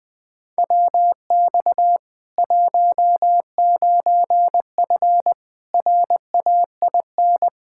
Morse code, heard as a keyed tone, about 20 wpm, 700 Hz. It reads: WX 19F RAIN
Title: K1USN SST QSO Practice 20wpm